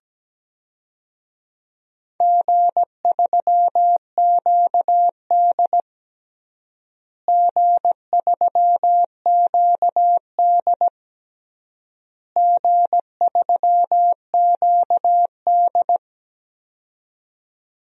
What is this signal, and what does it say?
Morse code, heard as a keyed tone, about 17 wpm, 700 Hz. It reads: G3QD G3QD G3QD
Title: DXCC-100 Call Signs - 4 Characters - Repeated 3x - 17wpm